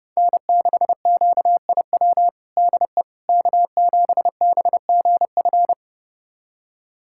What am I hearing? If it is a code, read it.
Morse code, heard as a keyed tone, about 30 wpm, 700 Hz. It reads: N6QIW DE K7BGF
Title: USA Callsign DE Callsign - 5 Characters - 30wpm